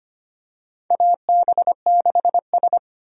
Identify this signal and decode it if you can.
Morse code, heard as a keyed tone, about 25 wpm, 700 Hz. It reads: AB6S
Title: USA Call Signs - 4 Characters - 25wpm